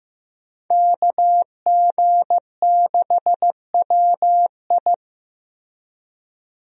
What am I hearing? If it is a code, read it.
Morse code, heard as a keyed tone, about 15 wpm, 700 Hz. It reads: KG6WI
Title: USA Call Signs - 5 Characters - Repeated 3x - 15wpm